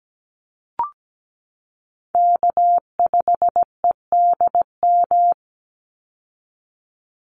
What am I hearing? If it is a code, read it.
Morse code, heard as a keyed tone, about 17 wpm, 700 Hz. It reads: K5EDM
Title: Call Signs Repeated 3x - 17wpm